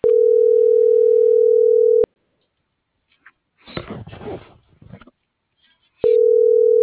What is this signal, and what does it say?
Ringback tone (North America), 440 + 480 Hz.